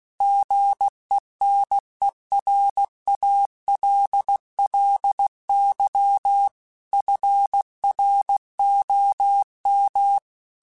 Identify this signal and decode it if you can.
Morse code, heard as a keyed tone, about 16 wpm, 800 Hz. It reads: GENERALLY FROM
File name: Weather Sets / Weather Sets-mp3s / Weather 4 17 @15